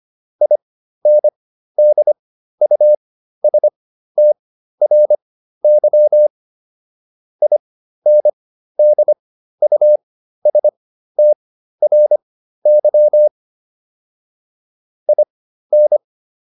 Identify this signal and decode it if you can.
Morse code, heard as a keyed tone, about 25 wpm, spaced as for 13 wpm, 600 Hz. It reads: INDUSTRY INDUSTRY IN